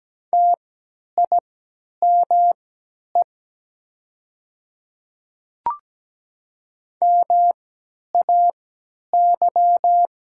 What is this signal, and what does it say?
Morse code, heard as a keyed tone, about 17 wpm, spaced as for 10 wpm, 700 Hz. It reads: TIME MAY
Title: 3x Character Spacing - Top 100 Words - 17wpm